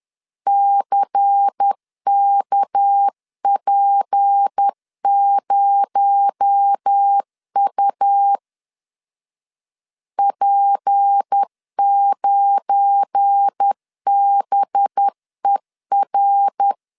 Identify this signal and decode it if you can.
Morse code, heard as a keyed tone, about 10 wpm, 800 Hz. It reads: CKP0U P9BER
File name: LEZIONI / 7 GF9Z / 7 GF9Z 10wpm 2611